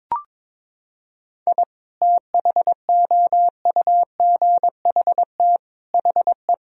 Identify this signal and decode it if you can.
Morse code, heard as a keyed tone, about 21 wpm, 700 Hz. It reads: I THOUGHT HE
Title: Sentences from Top 1000 Words - Review Part 4 - 22wpm